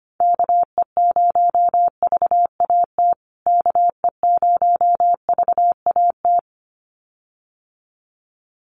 Morse code, heard as a keyed tone, about 25 wpm, 700 Hz. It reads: KE0VAT KE0VAT
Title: USA Call Signs Repeated 2x 25wpm